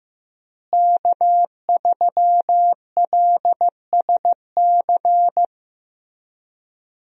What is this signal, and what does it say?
Morse code, heard as a keyed tone, about 15 wpm, 700 Hz. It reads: K3LSC
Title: USA Call Signs - 5 Characters - Repeated 3x - 15wpm